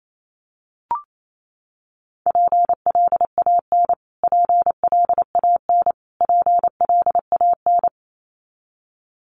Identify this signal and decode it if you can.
Morse code, heard as a keyed tone, about 28 wpm, 700 Hz. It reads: PLAN PLAN PLAN
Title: Top 500 Words Repeated 28wpm